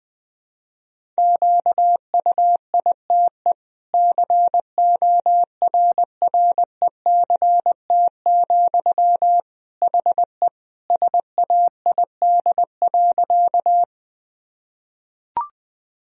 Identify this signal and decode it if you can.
Morse code, heard as a keyed tone, about 20 wpm, 700 Hz. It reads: QUITE CORRECT, HE SAID.
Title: Sentences from Top 1000 Words - Review Part 2 - 20wpm